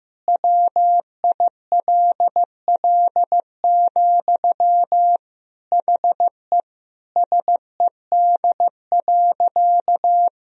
Morse code, heard as a keyed tone, about 15 wpm, 700 Hz. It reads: WILL, HE SED.
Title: Sentences from Top 200 Words and Abbreviations 15wpm